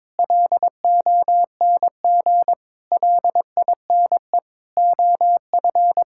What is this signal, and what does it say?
Morse code, heard as a keyed tone, about 22 wpm, 700 Hz. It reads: LONG LINE OF
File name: practice / 22wpm / Sets of 8 Words - 22wpm